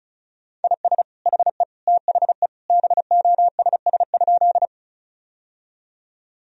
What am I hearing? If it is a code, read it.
Morse code, heard as a keyed tone, about 36 wpm, 700 Hz. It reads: IS HE THE BOSS?
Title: Sentences from Top 1000 Words - Review Part 4 - 35wpm